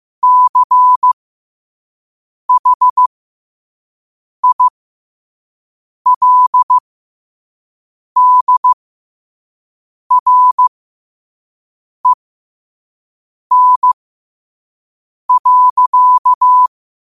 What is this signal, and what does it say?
Morse code, heard as a keyed tone, about 15 wpm, spaced as for 5 wpm, 1000 Hz. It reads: CHILDREN.
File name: qotdfiles / 5wpm